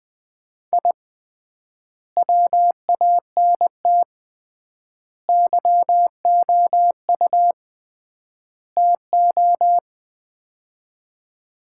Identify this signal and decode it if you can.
Morse code, heard as a keyed tone, about 20 wpm, 700 Hz. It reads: I WANT YOU TO
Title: 3x Word-Spacing - Sets of 4 Words - 20wpm